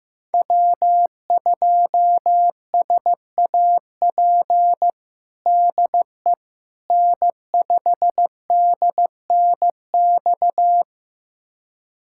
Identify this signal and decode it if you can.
Morse code, heard as a keyed tone, about 15 wpm, 700 Hz. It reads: W2SAP DE N5DNX
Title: USA Callsign DE Callsign - 5 Characters - 15wpm